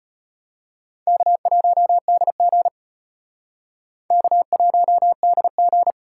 Morse code, heard as a keyed tone, about 31 wpm, 700 Hz. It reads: K1DG K1DG
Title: Speed Racing - USA 1x2 Call Signs 25wpm